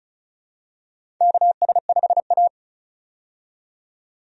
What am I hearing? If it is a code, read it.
Morse code, heard as a keyed tone, about 36 wpm, 700 Hz. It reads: KS5A
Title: USA Call Signs - 4 Characters - 35wpm